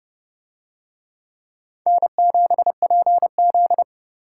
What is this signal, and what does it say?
Morse code, heard as a keyed tone, about 29 wpm, 700 Hz. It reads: N7PZ